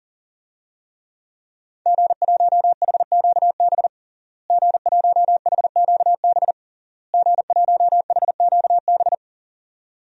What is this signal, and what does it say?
Morse code, heard as a keyed tone, about 40 wpm, 700 Hz. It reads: G1HQB G1HQB G1HQB
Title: DXCC-100 Call Signs - 5 Characters - Repeated 3x - 40wpm